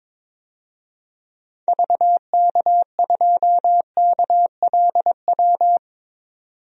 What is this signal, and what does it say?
Morse code, heard as a keyed tone, about 22 wpm, 700 Hz. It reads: VK2KLW
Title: DXCC-100 Call Signs - 6 Characters - 22wpm